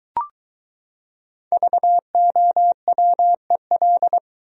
Morse code, heard as a keyed tone, about 22 wpm, 700 Hz. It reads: VOWEL